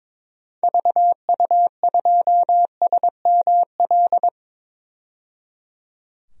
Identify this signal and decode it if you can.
Morse code, heard as a keyed tone, about 22 wpm, 700 Hz. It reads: VU2SML
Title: DXCC-100 Call Signs - 6 Characters - 22wpm